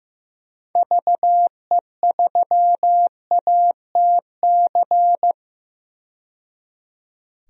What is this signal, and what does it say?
Morse code, heard as a keyed tone, about 15 wpm, 700 Hz. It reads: VE3ATC
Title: DXCC-100 Call Signs - 6 Characters - 15wpm